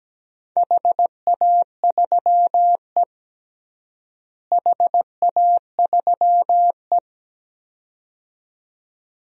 Morse code, heard as a keyed tone, about 17 wpm, 700 Hz. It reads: HA3E HA3E
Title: DXCC-100 Call Signs - 4 Characters - Repeated 2x - 17wpm